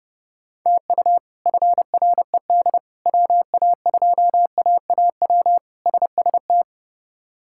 Morse code, heard as a keyed tone, about 30 wpm, 700 Hz. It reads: TU FRED WA2AAW SST
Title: K1USN SST QSO Practice - 30wpm